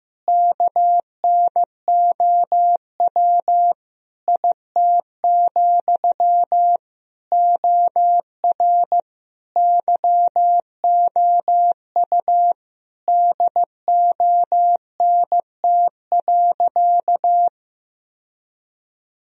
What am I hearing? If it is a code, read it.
Morse code, heard as a keyed tone, about 16 wpm, 700 Hz. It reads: KNOW IT, OR YOU DONT.
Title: Sentences from Top 1000 Words - Review Part 4 - 15wpm